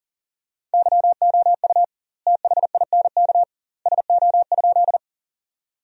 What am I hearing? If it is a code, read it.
Morse code, heard as a keyed tone, about 40 wpm, 700 Hz. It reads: YOU THINK SO?
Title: 2x Word-Spacing - Sentences from Top 200 Words - 40wpm